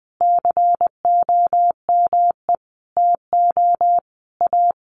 Morse code, heard as a keyed tone, about 20 wpm, 700 Hz. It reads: COME TO A
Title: Sentences from Top 1500 Words 20wpm